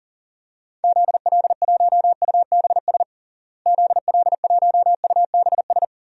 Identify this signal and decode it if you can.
Morse code, heard as a keyed tone, about 40 wpm, 700 Hz. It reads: ZL1UBS ZL1UBS
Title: DXCC-100 Call Signs - 6 Characters - Repeated 3x - 40wpm